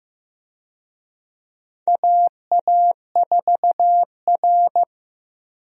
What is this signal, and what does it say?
Morse code, heard as a keyed tone, about 15 wpm, 700 Hz. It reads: AA4R
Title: USA Call Signs - 4 Characters - Repeated 2x - 15wpm